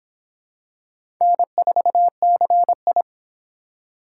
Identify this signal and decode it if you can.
Morse code, heard as a keyed tone, about 25 wpm, 700 Hz. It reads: N4CI